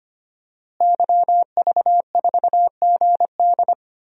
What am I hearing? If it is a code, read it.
Morse code, heard as a keyed tone, about 25 wpm, 700 Hz. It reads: YV4GD
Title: DXCC-100 Call Signs - 5 Characters - Repeated 2x - 25wpm